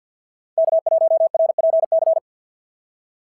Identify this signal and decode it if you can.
Morse code, heard as a keyed tone, about 53 wpm, 690 Hz. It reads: K1RPC
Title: SOTA QSO Practice 50wpm